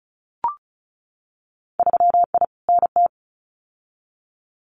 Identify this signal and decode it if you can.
Morse code, heard as a keyed tone, about 34 wpm, 700 Hz. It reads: 3I NT
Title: Winter Field Day Exchanges Encore 35wpm